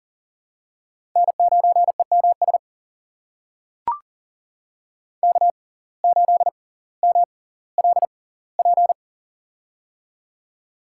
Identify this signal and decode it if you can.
Morse code, heard as a keyed tone, about 40 wpm, spaced as for 20 wpm, 700 Hz. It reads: N9EMS K8MLP
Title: USA Call Signs - 5 Characters - 6x Character Spacing - 40wpm